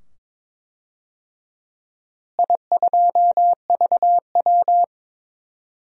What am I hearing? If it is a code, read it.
Morse code, heard as a keyed tone, about 22 wpm, 700 Hz. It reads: I2VW
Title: DXCC-100 Call Signs - 4 Characters - Repeated 2x - 22wpm